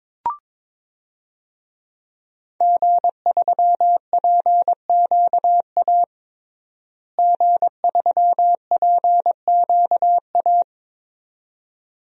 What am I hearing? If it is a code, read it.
Morse code, heard as a keyed tone, about 22 wpm, 700 Hz. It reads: G3PQA G3PQA
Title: DXCC-100 Call Signs - 5 Characters - Repeated 2x - 22wpm